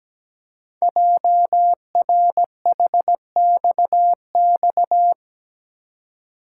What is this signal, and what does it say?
Morse code, heard as a keyed tone, about 17 wpm, 700 Hz. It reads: JRHXX
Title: Code Groups - 5 Letters - 17wpm